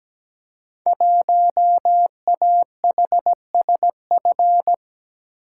Morse code, heard as a keyed tone, about 17 wpm, 700 Hz. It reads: 1AHSF
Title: Code Groups - 5 Letter:Numbers - 17wpm